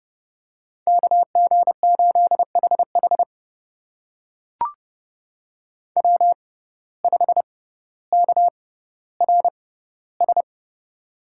Morse code, heard as a keyed tone, about 32 wpm, spaced as for 16 wpm, 700 Hz. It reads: KG8HH W5KRS
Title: USA Call Signs - 5 Characters - 6x Character Spacing - 30wpm